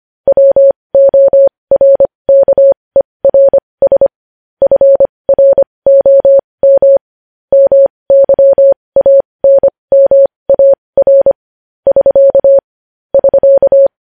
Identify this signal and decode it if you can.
Morse code, heard as a keyed tone, about 25 wpm, spaced as for 20 wpm, 550 Hz. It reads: WORKERS FROM MYANMAR <SK> <SK>
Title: News Headlines at 20